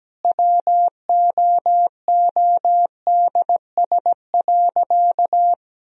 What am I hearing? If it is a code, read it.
Morse code, heard as a keyed tone, about 17 wpm, 700 Hz. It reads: WOODS.
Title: Sentences from Top 1000 Words - Review Part 2 - 17wpm